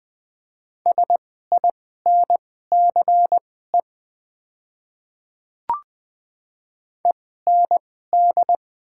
Morse code, heard as a keyed tone, about 20 wpm, spaced as for 15 wpm, 700 Hz. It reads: SINCE END
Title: 2x Character Spacing - Top 300 Words - 20wpm